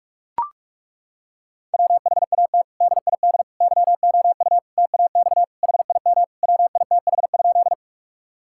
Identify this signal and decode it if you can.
Morse code, heard as a keyed tone, about 41 wpm, 700 Hz. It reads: WHAT DID YOU TAX HIM WITH?
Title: Sentences from Top 1000 Words - Review Part 5 - 45wpm